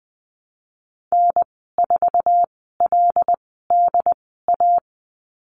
Morse code, heard as a keyed tone, about 20 wpm, spaced as for 15 wpm, 700 Hz. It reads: N4LDA
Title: 186 Lesson 30g - Letter b - Call Signs 20wpm